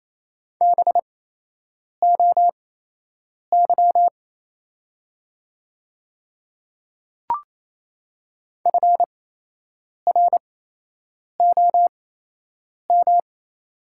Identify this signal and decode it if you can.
Morse code, heard as a keyed tone, about 28 wpm, spaced as for 8 wpm, 700 Hz. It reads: BOY FROM
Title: 8x Character Spacing - Top 200 Words - 28wpm